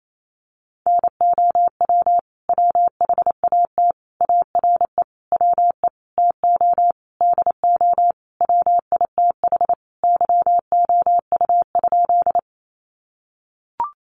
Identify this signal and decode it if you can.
Morse code, heard as a keyed tone, about 28 wpm, 700 Hz. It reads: NOW WHAT ARE WE TO DO WITH YOU?
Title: Sentences from Top 100 Words 28wpm